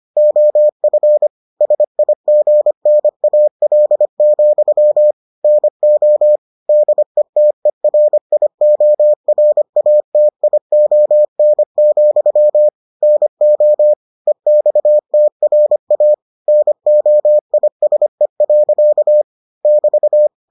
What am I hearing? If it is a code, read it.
Morse code, heard as a keyed tone, about 25 wpm, 600 Hz. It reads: OF SIGNAL, NO DETERIORATION, NO EXTRA NOISE. <BT>